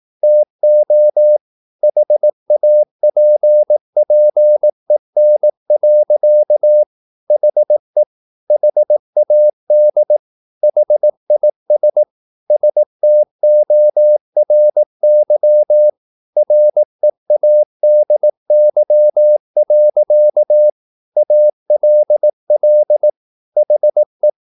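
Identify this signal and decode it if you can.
Morse code, heard as a keyed tone, about 18 wpm, 600 Hz. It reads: TO HAPPEN. HE HAD HIS STORY READY. ALL HE
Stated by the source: Morse practice at 18 wpm, English text, from CW med Gnister.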